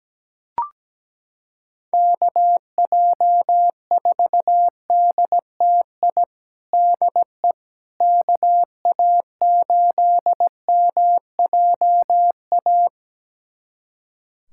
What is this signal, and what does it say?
Morse code, heard as a keyed tone, about 17 wpm, 700 Hz. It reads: KJ4DTI DE KA8MJA
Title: USA Callsign DE Callsign 17wpm